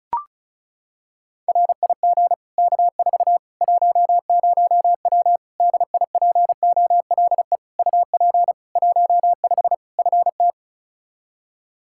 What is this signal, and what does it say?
Morse code, heard as a keyed tone, about 36 wpm, 700 Hz. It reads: RIG K4 10W DIPOLE UP 15 FT
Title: Practice QSO - Exchange 2 - Info - 35wpm